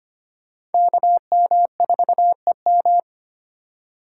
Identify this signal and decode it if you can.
Morse code, heard as a keyed tone, about 34 wpm, spaced as for 30 wpm, 700 Hz. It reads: KM4EM
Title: Speed Racing - USA 2x2 Call Signs 25wpm